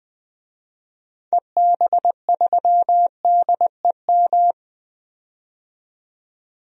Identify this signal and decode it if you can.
Morse code, heard as a keyed tone, about 20 wpm, 700 Hz. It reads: EB3DEM
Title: DXCC-100 Call Signs - 6 Characters - 20wpm